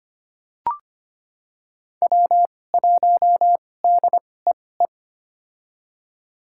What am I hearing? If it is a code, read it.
Morse code, heard as a keyed tone, about 25 wpm, spaced as for 18 wpm, 700 Hz. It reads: W1DEE
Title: USA Call Signs - 2x Character Spacing - 25wpm